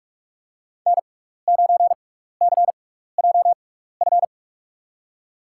Morse code, heard as a keyed tone, about 47 wpm, spaced as for 16 wpm, 700 Hz. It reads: N9CJF
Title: USA Call Signs - 5 Characters - 6x Character Spacing - 45wpm